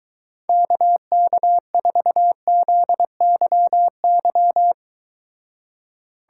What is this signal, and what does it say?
Morse code, heard as a keyed tone, about 23 wpm, 700 Hz. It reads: KK4ZYY